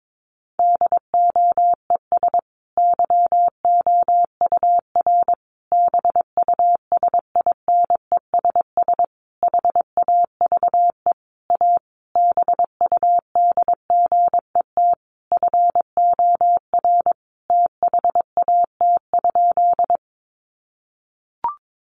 Morse code, heard as a keyed tone, about 22 wpm, 700 Hz. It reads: DOES YOUR BUSINESS HAVE A BUDGET FOR THAT?
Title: Sentences from Top 1200 Words 22wpm